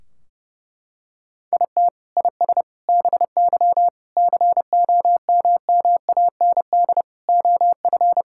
Morse code, heard as a keyed tone, about 32 wpm, 700 Hz. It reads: IT IS BY COMMAND OF
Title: Sentences from Top 1600 Words 30wpm